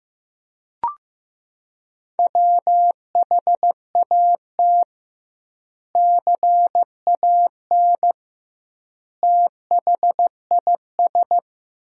Morse code, heard as a keyed tone, about 15 wpm, 700 Hz. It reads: WHAT CAN THIS
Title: 2x Word-Spacing - Sentences from Top 100 Words - 15wpm